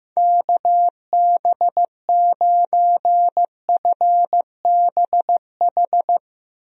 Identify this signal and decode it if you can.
Morse code, heard as a keyed tone, about 16 wpm, 700 Hz. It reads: KB9FBH
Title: USA Call Signs - 6 Characters - Repeated 2x - 15wpm